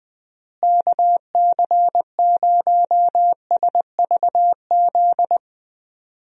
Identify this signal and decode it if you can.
Morse code, heard as a keyed tone, about 20 wpm, 700 Hz. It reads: KC0SVZ
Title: USA Call Signs - 6 Characters - Repeated 3x - 20wpm